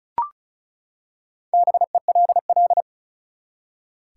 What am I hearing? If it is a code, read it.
Morse code, heard as a keyed tone, about 36 wpm, 700 Hz. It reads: BELL
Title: Four Letter Words - 35wpm